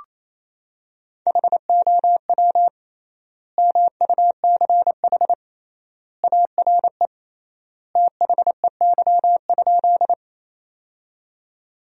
Morse code, heard as a keyed tone, about 27 wpm, 700 Hz. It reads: HOW MUCH ARE THEY?
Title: 3x Word-Spacing - Sentences from Top 200 Words - 28wpm